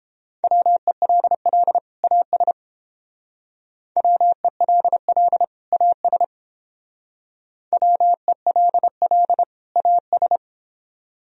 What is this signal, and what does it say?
Morse code, heard as a keyed tone, about 30 wpm, 700 Hz. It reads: WELL AS WELL AS WELL AS